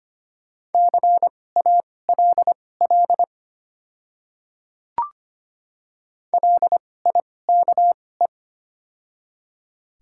Morse code, heard as a keyed tone, about 25 wpm, spaced as for 18 wpm, 700 Hz. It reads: CALL LIKE
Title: 2x Character Spacing - Top 200 Words - 25wpm